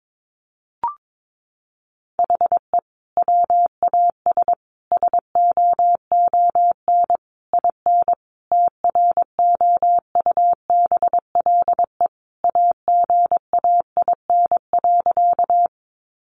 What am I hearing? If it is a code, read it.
Morse code, heard as a keyed tone, about 22 wpm, 700 Hz. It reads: HE WAS SOON IN TROUBLE AGAIN.
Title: Sentences from Top 1200 Words 22wpm